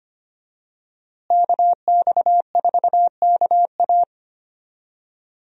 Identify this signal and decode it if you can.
Morse code, heard as a keyed tone, about 25 wpm, 700 Hz. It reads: KX4KA
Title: USA Call Signs - 5 Characters - Repeated 2x - 25wpm